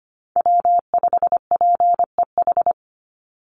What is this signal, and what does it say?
Morse code, heard as a keyed tone, about 25 wpm, 700 Hz. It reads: W5PEH
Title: Call Signs Repeated 2x - 25wpm